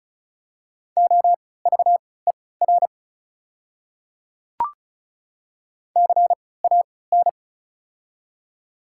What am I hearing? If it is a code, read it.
Morse code, heard as a keyed tone, about 36 wpm, spaced as for 20 wpm, 700 Hz. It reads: OVER CAN
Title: 3x Character Spacing - Top 100 Words - 35wpm